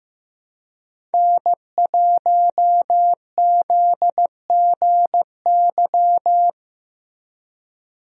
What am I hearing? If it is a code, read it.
Morse code, heard as a keyed tone, about 15 wpm, 700 Hz. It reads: N1ZGY
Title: USA Call Signs - 5 Characters - Repeated 2x - 15wpm